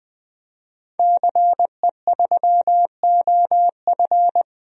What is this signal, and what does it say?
Morse code, heard as a keyed tone, about 20 wpm, 700 Hz. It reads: CE3OF
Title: DXCC-100 Call Signs - 5 Characters - Repeated 2x - 20wpm